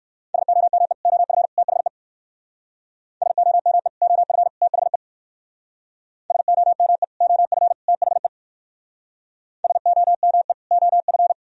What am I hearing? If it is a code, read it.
Morse code, heard as a keyed tone, about 61 wpm, 700 Hz. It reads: SOME OF THE SOME OF THE SOME OF THE SOME OF